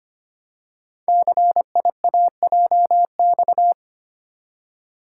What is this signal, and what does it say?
Morse code, heard as a keyed tone, about 25 wpm, 700 Hz. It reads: CIAJX
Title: Code Groups - 5 Letters - 25wpm